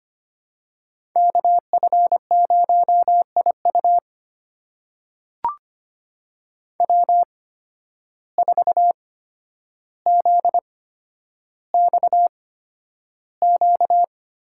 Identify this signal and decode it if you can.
Morse code, heard as a keyed tone, about 25 wpm, spaced as for 11 wpm, 700 Hz. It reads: KF0IU W4ZXQ
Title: USA Call Signs - 5 Characters - 8x Character Spacing - 25wpm